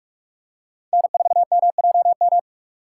Morse code, heard as a keyed tone, about 43 wpm, 700 Hz. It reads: N4MJM
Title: USA Call Signs - 5 Characters - 4x Character Spacing - 45wpm